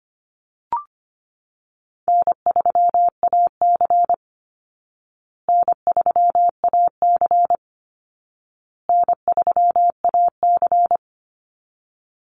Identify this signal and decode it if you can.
Morse code, heard as a keyed tone, about 25 wpm, 700 Hz. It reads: N3AC N3AC N3AC
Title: Call Signs Repeated 2x - 25wpm